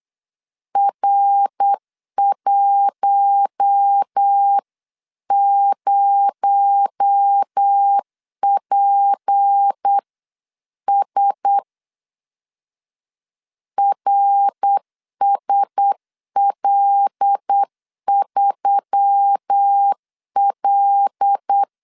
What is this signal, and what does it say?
Morse code, heard as a keyed tone, about 8 wpm, 800 Hz. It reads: R10PS RSL3L
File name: 8wpm